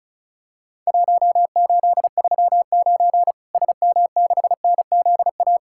Morse code, heard as a keyed tone, about 36 wpm, 700 Hz. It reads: 1839 SM6NZA
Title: Worked All Europe DX Contest - QTC Exchange - 35wpm